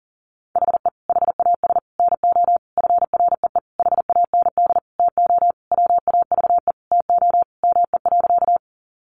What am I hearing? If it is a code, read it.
Morse code, heard as a keyed tone, about 40 wpm, 700 Hz. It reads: HE HAS NO FREE HAND TO WAVE TO ME.
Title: Sentences from Top 1400 Words 40wpm